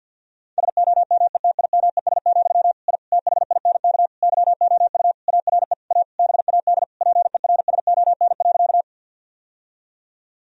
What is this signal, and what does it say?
Morse code, heard as a keyed tone, about 52 wpm, 700 Hz. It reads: SOMETIMES, I THINK YOU ARE A BAD PERSON.
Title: Sentences from Top 1000 Words - Review Part 3 - 50wpm